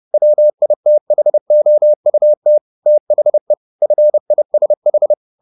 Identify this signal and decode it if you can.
Morse code, heard as a keyed tone, about 30 wpm, 600 Hz. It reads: WITHOUT THE FISH
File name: the old man and the sea / the old man and the sea 1 30wpm